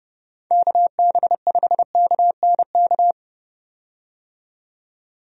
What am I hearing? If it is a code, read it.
Morse code, heard as a keyed tone, about 30 wpm, 700 Hz. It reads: KB5KNK
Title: USA Call Signs - 6 Characters - 30wpm